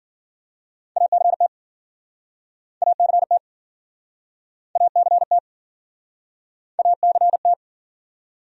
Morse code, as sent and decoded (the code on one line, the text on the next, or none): .- -.-. - / .- -.-. - / .- -.-. - / .- -.-. -
ACT ACT ACT ACT